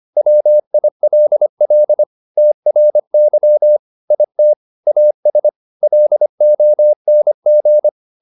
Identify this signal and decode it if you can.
Morse code, heard as a keyed tone, about 25 wpm, 600 Hz. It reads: WILL TRY IT AS LONG